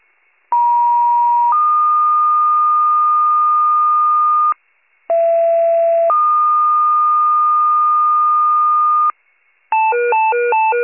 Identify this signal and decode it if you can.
Two-tone page, heard as a one-second tone, then a three-second tone, about 950 then 1220 Hz, about 670 then 1220 Hz. It is heard from 0.5 to 9.1 s.